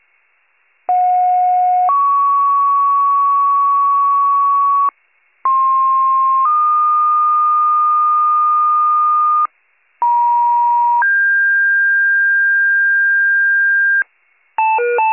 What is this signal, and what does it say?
Two-tone page, heard as a one-second tone, then a three-second tone, about 730 then 1120 Hz, about 1060 then 1220 Hz, about 950 then 1600 Hz. It is heard from 0.9 to 14.0 s.